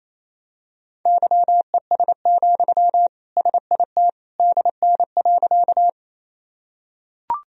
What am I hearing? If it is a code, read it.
Morse code, heard as a keyed tone, about 28 wpm, 700 Hz. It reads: YES, SIT DN.
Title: Sentences from Top 500 Words and Abbreviations 28wpm